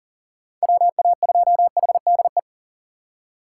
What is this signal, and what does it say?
Morse code, heard as a keyed tone, about 40 wpm, 700 Hz. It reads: WA2HDE
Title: USA Call Signs - 6 Characters - 6x Character Spacing - 40wpm